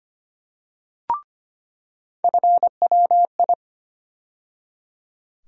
Morse code, heard as a keyed tone, about 25 wpm, 700 Hz. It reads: FWI
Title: USA Call Sign Postfixes - 3 Letters 25wpm